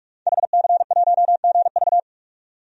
Morse code, heard as a keyed tone, about 45 wpm, 700 Hz. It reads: HC1GV
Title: DXCC-100 Call Signs - 5 Characters - 45wpm